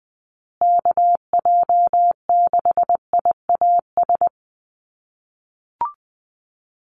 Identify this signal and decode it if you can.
Morse code, heard as a keyed tone, about 20 wpm, 700 Hz. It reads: KJ6IAS
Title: USA Call Signs Repeated 2x 20wpm